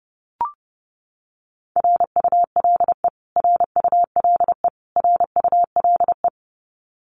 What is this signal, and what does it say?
Morse code, heard as a keyed tone, about 30 wpm, 700 Hz. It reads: RULE RULE RULE
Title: Top 500 Words Repeated 30wpm